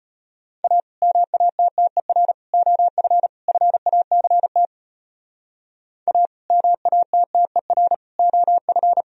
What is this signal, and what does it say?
Morse code, heard as a keyed tone, about 34 wpm, 700 Hz. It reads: A MATTER OF FACT A MATTER OF